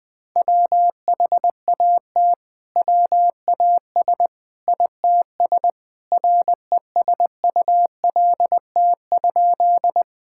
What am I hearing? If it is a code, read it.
Morse code, heard as a keyed tone, about 20 wpm, 700 Hz. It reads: WHAT WAS ITS RESULT?
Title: Sentences from Top 1000 Words - Review Part 5 - 20wpm